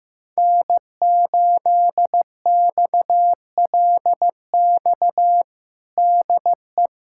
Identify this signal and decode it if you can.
Morse code, heard as a keyed tone, about 15 wpm, 700 Hz. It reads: N8XLX DE
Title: USA Callsign DE Callsign - 5 Characters - 15wpm